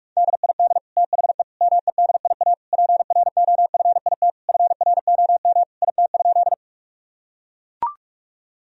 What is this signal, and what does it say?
Morse code, heard as a keyed tone, about 44 wpm, 700 Hz. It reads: DID THE MEDIA PROFIT FROM IT?
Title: Sentences from Top 1000 Words - Review Part 4 - 45wpm